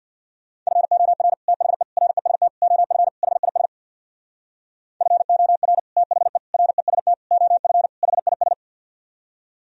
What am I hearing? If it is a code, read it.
Morse code, heard as a keyed tone, about 64 wpm, 700 Hz. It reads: FOR THE REST OF HIS FOR THE REST OF HIS